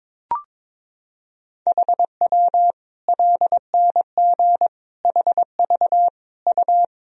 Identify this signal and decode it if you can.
Morse code, heard as a keyed tone, about 22 wpm, 700 Hz. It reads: HW LNG HV U
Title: Sentences from Top 200 Words and Abbreviations 22wpm